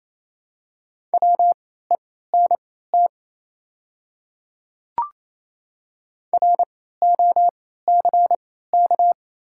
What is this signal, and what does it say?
Morse code, heard as a keyed tone, about 28 wpm, spaced as for 16 wpm, 700 Hz. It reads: WENT ROCK
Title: 3x Character Spacing - Top 500 Words - 28wpm